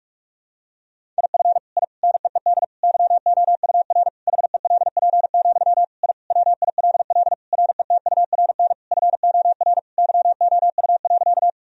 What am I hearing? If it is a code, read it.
Morse code, heard as a keyed tone, about 41 wpm, 700 Hz. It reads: IF I NEED YOUR HELP, I WILL RETURN FOR YOU.
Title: Sentences from Top 1000 Words - Review Part 1 - 45wpm